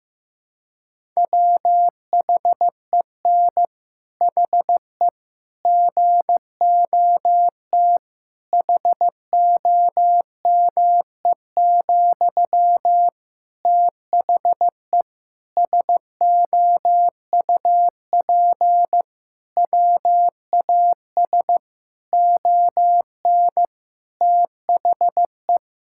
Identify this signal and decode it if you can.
Morse code, heard as a keyed tone, about 15 wpm, 700 Hz. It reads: WHEN HE GOT HOME, THE SOUP WAS ON THE
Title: Sentences from Top 1000 Words - Review Part 6 - 15wpm